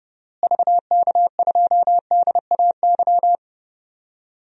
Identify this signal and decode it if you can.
Morse code, heard as a keyed tone, about 30 wpm, 700 Hz. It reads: VK2DAY
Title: DXCC-100 Call Signs - 6 Characters - Repeated 3x - 30wpm